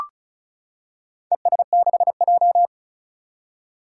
Morse code, heard as a keyed tone, about 36 wpm, 700 Hz. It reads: ES6J
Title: Four Challenging Letter-Number Combinations 35wpm